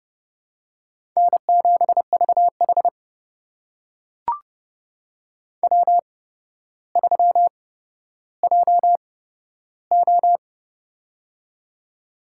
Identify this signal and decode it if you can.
Morse code, heard as a keyed tone, about 28 wpm, spaced as for 13 wpm, 700 Hz. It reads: N7VH W3JO
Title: USA Call Signs - 4 Characters - 8x Character Spacing - 30wpm